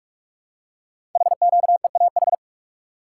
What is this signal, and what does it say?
Morse code, heard as a keyed tone, about 44 wpm, 690 Hz. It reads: HQEAH
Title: Code Groups - 5 Letters - 45wpm